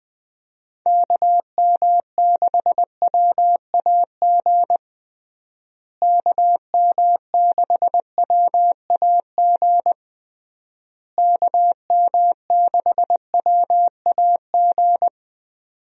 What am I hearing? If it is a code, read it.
Morse code, heard as a keyed tone, about 20 wpm, 700 Hz. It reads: KM6WAG KM6WAG KM6WAG
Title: USA Call Signs - 6 Characters - Repeated 3x - 20wpm